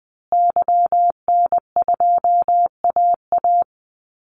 Morse code, heard as a keyed tone, about 20 wpm, 700 Hz. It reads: YN2AA
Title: Call Signs Repeated 3x - 20wpm